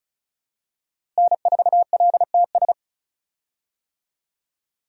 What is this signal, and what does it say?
Morse code, heard as a keyed tone, about 34 wpm, 700 Hz. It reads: N4LTS
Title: USA Call Signs - 5 Characters - Repeated 2x - 35wpm